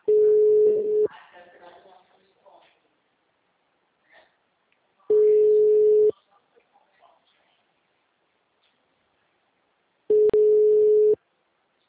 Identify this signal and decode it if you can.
Ringback tone (Europe, 425 Hz), with speech.